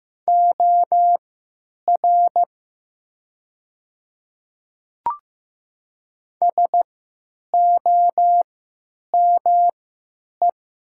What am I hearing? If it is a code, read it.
Morse code, heard as a keyed tone, about 15 wpm, spaced as for 9 wpm, 700 Hz. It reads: OR SOME
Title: 3x Character Spacing - Top 100 Words - 15wpm